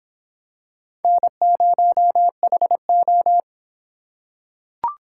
Morse code, heard as a keyed tone, about 26 wpm, 700 Hz. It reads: N0HO